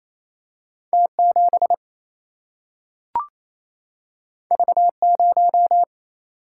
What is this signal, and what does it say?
Morse code, heard as a keyed tone, about 28 wpm, 700 Hz. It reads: T7 V0
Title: Single-Letter Single-Number 28wpm